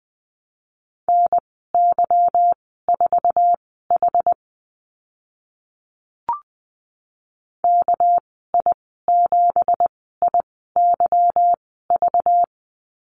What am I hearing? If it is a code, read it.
Morse code, heard as a keyed tone, about 20 wpm, spaced as for 15 wpm, 700 Hz. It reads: NY4H KI7IYV
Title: 213 Lesson 34g - Number 8 - Call Signs 20wpm